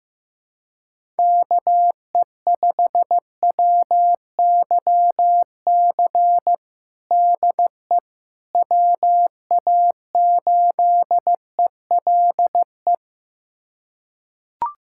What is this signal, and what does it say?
Morse code, heard as a keyed tone, about 15 wpm, 700 Hz. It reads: KE5WYC DE WA8ELE
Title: USA Callsign DE Callsign - 6 Characters - 15wpm